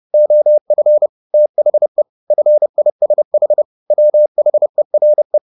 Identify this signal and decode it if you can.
Morse code, heard as a keyed tone, about 30 wpm, 600 Hz. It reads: OF THE FISH WHERE